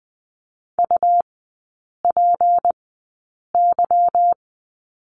Morse code, heard as a keyed tone, about 20 wpm, spaced as for 8 wpm, 700 Hz. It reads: UPY
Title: 152 Lesson 25e - Letter p - Three Characters 20wpm